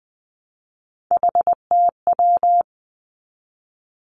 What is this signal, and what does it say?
Morse code, heard as a keyed tone, about 20 wpm, 700 Hz. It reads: HTW
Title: Three Challenging Letter-Number Combinations 20wpm